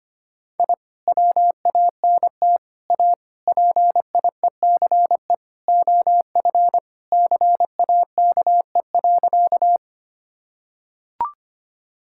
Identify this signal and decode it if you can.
Morse code, heard as a keyed tone, about 25 wpm, 700 Hz. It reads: I WANT A PIECE OF CAKE.
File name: practice / 25wpm / Sentences with 3-Word Phrases - 25wpm